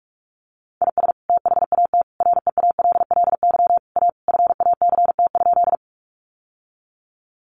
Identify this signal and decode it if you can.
Morse code, heard as a keyed tone, about 44 wpm, 700 Hz. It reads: IS THAT REALLY A FACT?
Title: Sentences from Top 600 Words 45wpm